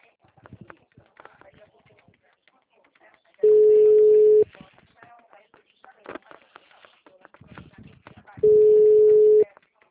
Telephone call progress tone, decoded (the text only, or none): ringback tone